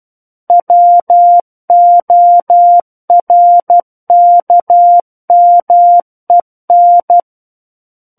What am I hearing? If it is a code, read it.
Morse code, heard as a keyed tone, about 12 wpm, 700 Hz. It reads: WORKMEN
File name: Book One / 03-Chapter 3 - 12 WPM